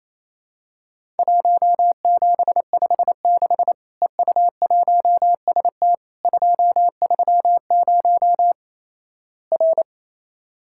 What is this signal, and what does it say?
Morse code, heard as a keyed tone, about 28 wpm, 700 Hz. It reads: 1756 EU1ST 230 R
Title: Worked All Europe DX Contest - QTC Exchange - 28wpm